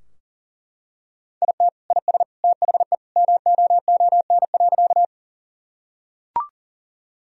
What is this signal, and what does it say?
Morse code, heard as a keyed tone, about 40 wpm, 700 Hz. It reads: IT IS THE MOON.
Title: Sentences from Top 1000 Words - Review Part 3 - 40wpm